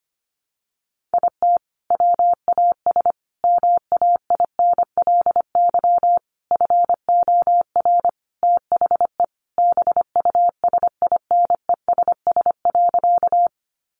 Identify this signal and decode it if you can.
Morse code, heard as a keyed tone, about 25 wpm, 700 Hz. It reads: IT WAS MAINLY FOR THE BUSINESS.
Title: Sentences from Top 1400 Words 25wpm